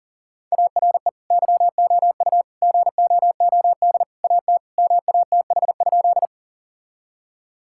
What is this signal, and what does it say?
Morse code, heard as a keyed tone, about 40 wpm, 700 Hz. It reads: ARE YOU GOOD AT MATH?
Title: Sentences from Top 1000 Words - Review Part 6 - 40wpm